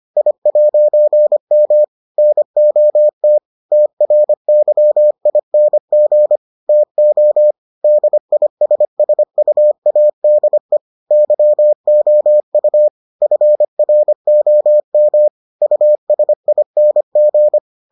Morse code, heard as a keyed tone, about 25 wpm, 600 Hz. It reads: I'M NOT TRYING TO DISSUADE YOU FROM USING